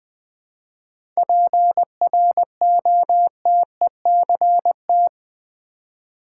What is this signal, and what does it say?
Morse code, heard as a keyed tone, about 20 wpm, 700 Hz. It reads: PROTECT
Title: 7 Letter Words - First 3 Letters Given - 20wpm